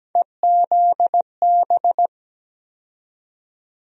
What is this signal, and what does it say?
Morse code, heard as a keyed tone, about 17 wpm, 700 Hz. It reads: EZB
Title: USA Call Sign Postfixes - 3 Letters 17wpm